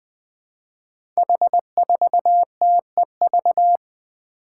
Morse code, heard as a keyed tone, about 20 wpm, 700 Hz. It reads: H4TEV
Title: Five Challenging Letter-Number Combinations 20wpm